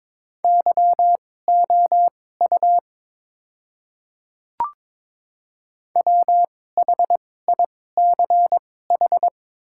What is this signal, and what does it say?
Morse code, heard as a keyed tone, about 22 wpm, spaced as for 16 wpm, 700 Hz. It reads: YOU WHICH
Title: 2x Character Spacing - Top 100 Words - 22wpm